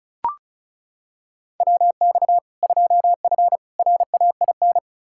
Morse code, heard as a keyed tone, about 34 wpm, 700 Hz. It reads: WX 2F RAIN
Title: K1USN SST QSO Practice 35wpm